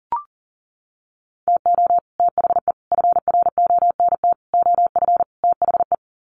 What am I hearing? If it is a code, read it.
Morse code, heard as a keyed tone, about 40 wpm, 700 Hz. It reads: TO THE FRONT OF THE
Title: Sets of 5 Words 40wpm